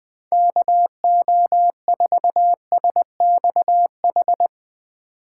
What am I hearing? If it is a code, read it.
Morse code, heard as a keyed tone, about 20 wpm, 700 Hz. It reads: KO4SXH
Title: 2023 POTA Hunter Call Signs - 20wpm